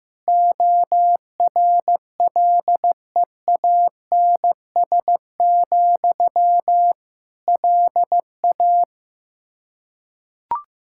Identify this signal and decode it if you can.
Morse code, heard as a keyed tone, about 15 wpm, 700 Hz. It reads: ORLEANS, LA
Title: K1USN SST QSO Practice 15wpm